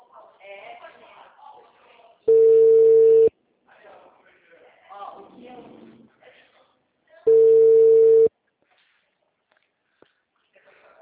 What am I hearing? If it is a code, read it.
Ringback tone (China, 450 Hz), with speech.
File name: out- -2002- - - .859